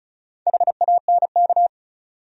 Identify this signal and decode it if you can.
Morse code, heard as a keyed tone, about 34 wpm, 700 Hz. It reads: HANK
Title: USA Common Names - Max 4 Letters - 35wpm